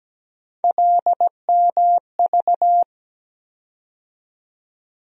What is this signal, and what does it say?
Morse code, heard as a keyed tone, about 17 wpm, 700 Hz. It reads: LMV
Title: USA Call Sign Postfixes - 3 Letters 17wpm